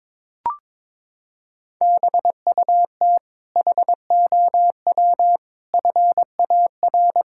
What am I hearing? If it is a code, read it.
Morse code, heard as a keyed tone, about 22 wpm, 700 Hz. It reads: BUT HOW FAR
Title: Sentences from Top 1000 Words - Review Part 6 - 22wpm